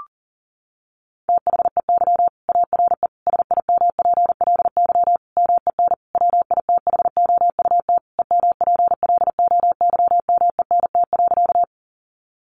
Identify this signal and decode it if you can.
Morse code, heard as a keyed tone, about 40 wpm, 700 Hz. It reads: THEY ARE SIMPLY MEN WITHOUT EMPLOYMENT.
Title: Sentences from Top 1500 Words 40wpm